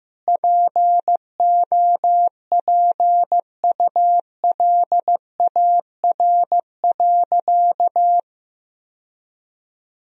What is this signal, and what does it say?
Morse code, heard as a keyed tone, about 15 wpm, 700 Hz. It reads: POPULAR.
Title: Sentences from Top 1000 Words - Review Part 3 - 15wpm